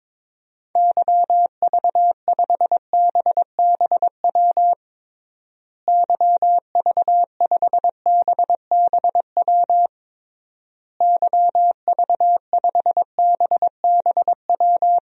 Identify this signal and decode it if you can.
Morse code, heard as a keyed tone, about 22 wpm, 700 Hz. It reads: YV5BBW YV5BBW YV5BBW
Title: DXCC-100 Call Signs - 6 Characters - Repeated 3x - 22wpm